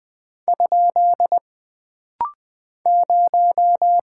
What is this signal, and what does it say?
Morse code, heard as a keyed tone, about 20 wpm, 700 Hz. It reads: ? 0
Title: Single Character (Letter:Number:Punctuation:Prosign) - Rapid-Fire - 20wpm